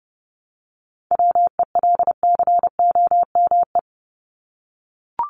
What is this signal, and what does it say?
Morse code, heard as a keyed tone, about 29 wpm, 700 Hz. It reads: WELCOME
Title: Top 1751-2000 Words 30wpm